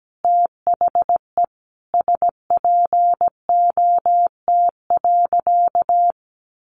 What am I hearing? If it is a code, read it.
Morse code, heard as a keyed tone, about 17 wpm, 700 Hz. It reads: THE SPOT.
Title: Sentences from Top 1200 Words 17wpm